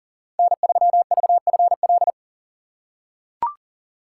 Morse code, heard as a keyed tone, about 40 wpm, 700 Hz. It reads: N3VFL
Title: USA Call Signs - 2x Character Spacing - 40wpm